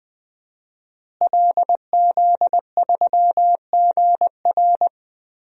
Morse code, heard as a keyed tone, about 21 wpm, 700 Hz. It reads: LZ3GR
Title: DXCC-100 Call Signs - 5 Characters - Repeated 3x - 20wpm